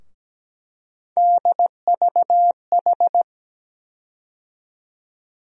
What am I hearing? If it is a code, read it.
Morse code, heard as a keyed tone, about 17 wpm, 700 Hz. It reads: DVH
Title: USA Call Sign Postfixes - 3 Letters 17wpm